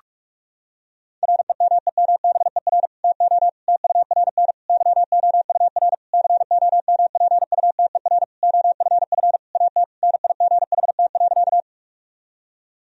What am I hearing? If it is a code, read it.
Morse code, heard as a keyed tone, about 42 wpm, 700 Hz. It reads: REMEMBER TO TURN YOUR COMPUTER OFF AT NIGHT.
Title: Sentences from Top 1000 Words - Review Part 3 - 45wpm